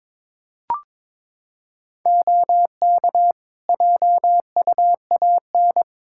Morse code, heard as a keyed tone, about 22 wpm, 700 Hz. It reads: OK JUAN
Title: Practice QSO - Exchange 3 - More Info - 22wpm